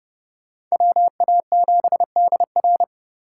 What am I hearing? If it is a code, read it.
Morse code, heard as a keyed tone, about 34 wpm, 700 Hz. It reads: WA7DR
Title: Speed Racing - USA 2x2 Call Signs 30wpm